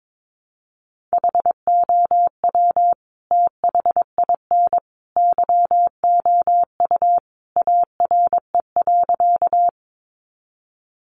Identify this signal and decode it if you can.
Morse code, heard as a keyed tone, about 22 wpm, 700 Hz. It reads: HOW THIN YOU ARE.
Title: Sentences from Top 1200 Words 22wpm